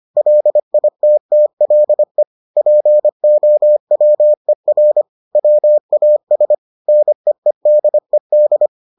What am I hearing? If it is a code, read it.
Morse code, heard as a keyed tone, about 25 wpm, 600 Hz. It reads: LITTLE POWER WAS NEEDED